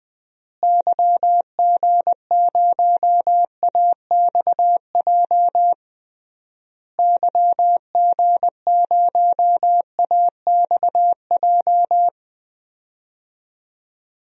Morse code, heard as a keyed tone, about 19 wpm, 700 Hz. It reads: YG0AXJ YG0AXJ
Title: DXCC-100 Call Signs - 6 Characters - Repeated 3x - 20wpm